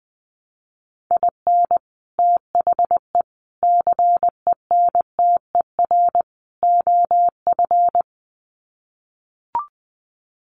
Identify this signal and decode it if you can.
Morse code, heard as a keyed tone, about 20 wpm, 700 Hz. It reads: IN THE CENTER OF
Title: Sets of 4 Words 20wpm